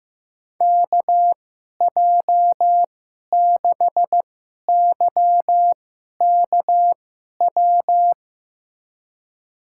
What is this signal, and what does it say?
Morse code, heard as a keyed tone, about 16 wpm, spaced as for 11 wpm, 700 Hz. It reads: KJ6YKW
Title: USA Call Signs - 2x Character Spacing - 15wpm